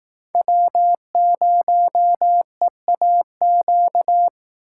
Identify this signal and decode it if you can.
Morse code, heard as a keyed tone, about 18 wpm, 700 Hz. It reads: W0EAQ
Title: Speed Racing - USA 1x3 Call Signs 15wpm